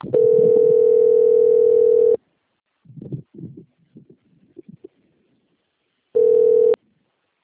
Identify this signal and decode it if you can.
Ringback tone (North America), 440 + 480 Hz.